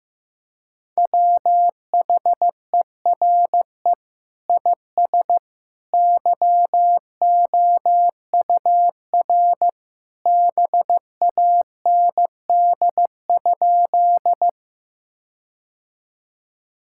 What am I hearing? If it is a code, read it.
Morse code, heard as a keyed tone, about 15 wpm, 700 Hz. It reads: WHERE IS YOUR BAND?
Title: Sentences from Top 1600 Words 15wpm